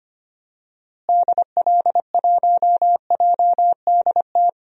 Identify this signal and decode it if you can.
Morse code, heard as a keyed tone, about 25 wpm, 700 Hz. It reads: DL1JDT
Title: DXCC-100 Call Signs - 6 Characters - Repeated 3x - 25wpm